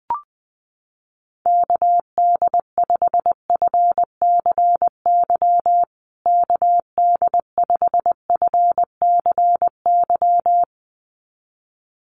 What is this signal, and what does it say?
Morse code, heard as a keyed tone, about 20 wpm, 700 Hz. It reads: KD5FCY KD5FCY
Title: USA Call Signs Repeated 2x 20wpm